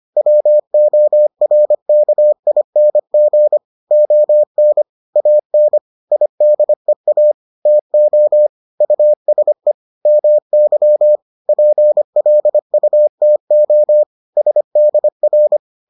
Morse code, heard as a keyed tone, about 25 wpm, 600 Hz. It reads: WORKING ON AN IDEA TO USE MY PLUTO SDR